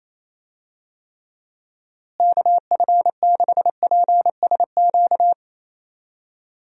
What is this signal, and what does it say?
Morse code, heard as a keyed tone, about 28 wpm, 700 Hz. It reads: KF6PSQ
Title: USA Call Signs - 6 Characters - Repeated 2x - 28wpm